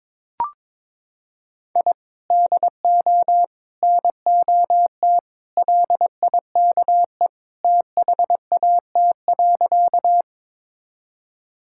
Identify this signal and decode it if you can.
Morse code, heard as a keyed tone, about 22 wpm, 700 Hz. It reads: I DO NOT LIKE THAT.
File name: practice / 22wpm / Sentences with 3-Word Phrases - 22wpm